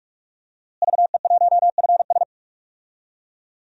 Morse code, heard as a keyed tone, about 46 wpm, 700 Hz. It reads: VE1FS
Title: DXCC-100 Call Signs - 5 Characters - Repeated 2x - 45wpm